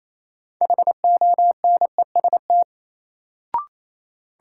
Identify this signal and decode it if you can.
Morse code, heard as a keyed tone, about 28 wpm, 700 Hz. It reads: HONEST
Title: Sets of 5 Words Encore - New Word Drill - 28wpm